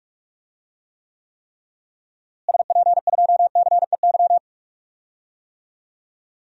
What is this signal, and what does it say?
Morse code, heard as a keyed tone, about 46 wpm, 700 Hz. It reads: SP2CEY
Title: DXCC-100 Call Signs - 6 Characters - Repeated 3x - 45wpm